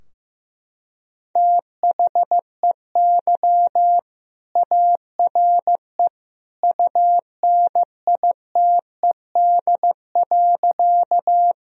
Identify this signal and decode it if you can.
Morse code, heard as a keyed tone, about 15 wpm, 700 Hz. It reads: THEY ARE UNITED.
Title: Sentences from Top 1000 Words - Review Part 4 - 15wpm